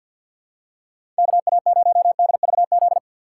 Morse code, heard as a keyed tone, about 49 wpm, 700 Hz. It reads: KA0DVZ
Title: USA Call Signs - 6 Characters - 3x Character Spacing - 50wpm